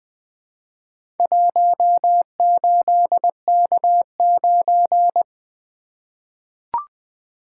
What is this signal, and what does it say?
Morse code, heard as a keyed tone, about 21 wpm, 700 Hz. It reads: 18K9
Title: Four Letter-Number Combinations - 20wpm